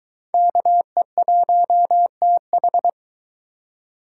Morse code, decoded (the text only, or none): KE1TH